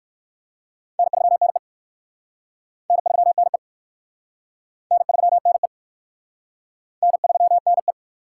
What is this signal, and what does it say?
Morse code, heard as a keyed tone, about 59 wpm, 700 Hz. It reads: N3NE N3NE N3NE N3NE